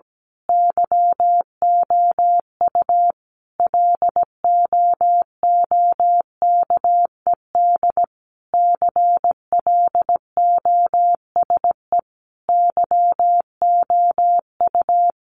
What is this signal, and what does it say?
Morse code, heard as a keyed tone, about 17 wpm, 700 Hz. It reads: YOU LOOKED CLOSE YOU
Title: A Sound Of Thunder 17wpm